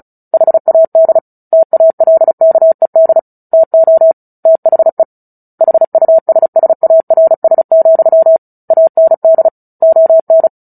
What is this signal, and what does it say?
Morse code, heard as a keyed tone, about 34 wpm, 650 Hz. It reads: HAD TALKED TO THE HUSSARS, AND ON
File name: Book Two / 09-Chapter 9 - 35 WPM